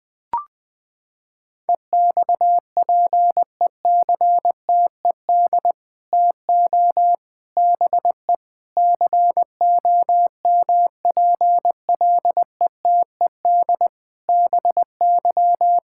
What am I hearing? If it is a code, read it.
Morse code, heard as a keyed tone, about 20 wpm, 700 Hz. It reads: EXPECTED TO BE COMPLETED BY
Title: Sets of 8 Words - 20wpm